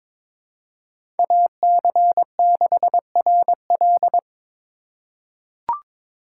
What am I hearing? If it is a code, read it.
Morse code, heard as a keyed tone, about 22 wpm, 700 Hz. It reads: AC6RL
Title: USA Call Signs - 5 Characters - 2x Character Spacing - 22wpm